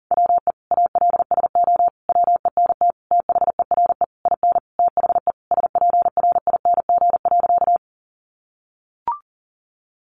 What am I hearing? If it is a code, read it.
Morse code, heard as a keyed tone, about 40 wpm, 700 Hz. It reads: WE ALSO WENT THERE IN THE SPRING.
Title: Sentences from Top 1200 Words 40wpm